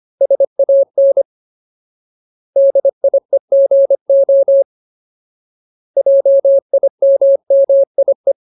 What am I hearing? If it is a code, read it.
Morse code, heard as a keyed tone, about 25 wpm, 550 Hz. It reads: SAN DIEGO JIMMIE
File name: Ultimate Word List / 25WPM Ultimate